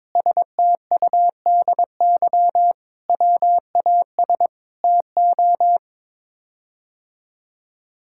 Morse code, heard as a keyed tone, about 22 wpm, 700 Hz. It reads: STUDY WAS TO
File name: practice / 22wpm / Sets of 7 Words - 22wpm